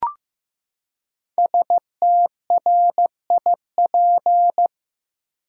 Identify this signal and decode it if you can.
Morse code, heard as a keyed tone, about 16 wpm, 700 Hz. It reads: STRIP
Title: Five Letter Words - 15wpm